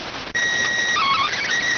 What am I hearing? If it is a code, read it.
SSTV (slow-scan television), PD 290.